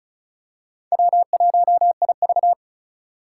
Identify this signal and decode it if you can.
Morse code, heard as a keyed tone, about 36 wpm, 700 Hz. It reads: W1IV
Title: Four Challenging Letter-Number Combinations 35wpm